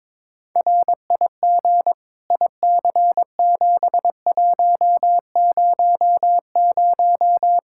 Morse code, heard as a keyed tone, about 22 wpm, 700 Hz. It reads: RIG IC7100
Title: Practice QSO - Exchange 2 - Info - 22wpm